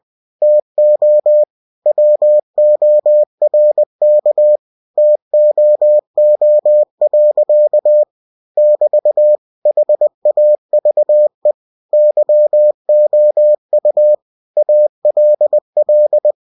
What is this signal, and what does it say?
Morse code, heard as a keyed tone, about 20 wpm, 600 Hz. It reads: TO WORK TOO. <BT> HAVE YOU ALL
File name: cw-books / winnie-the-pooh / winnie-20wpm-page-0083